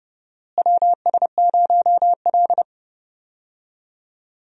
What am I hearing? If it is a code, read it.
Morse code, heard as a keyed tone, about 30 wpm, 700 Hz. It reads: WS0L
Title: USA Call Signs - 4 Characters - Repeated 3x - 30wpm